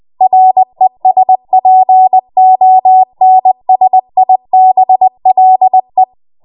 Morse code, heard as a keyed tone, about 20 wpm, 750 Hz. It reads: RESPONSIBLE
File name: Morse / 20WPM